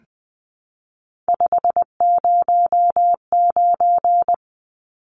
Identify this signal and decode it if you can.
Morse code, heard as a keyed tone, about 20 wpm, 700 Hz. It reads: 509
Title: Numbers - 3-Digit 20wpm